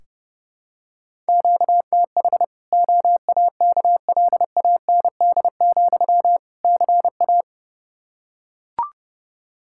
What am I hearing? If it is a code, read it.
Morse code, heard as a keyed tone, about 32 wpm, 700 Hz. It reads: QTH OAKLAND, CA
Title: K1USN SST QSO Practice 30wpm